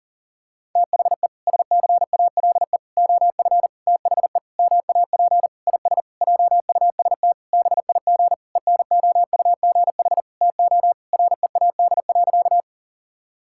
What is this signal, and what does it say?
Morse code, heard as a keyed tone, about 40 wpm, 700 Hz. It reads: THE SCALE OF THE MAP IS JUST BIG ENOUGH TO READ.
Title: Sentences from Top 1000 Words - Review Part 5 - 40wpm